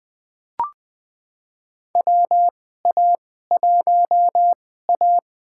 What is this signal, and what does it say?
Morse code, heard as a keyed tone, about 20 wpm, spaced as for 14 wpm, 700 Hz. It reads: WA1A
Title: USA Call Signs - 4 Characters - 2x Character Spacing - 20wpm